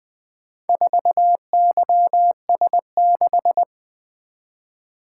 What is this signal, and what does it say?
Morse code, heard as a keyed tone, about 20 wpm, 700 Hz. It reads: 4YS6
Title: Four Letter-Number Combinations - 20wpm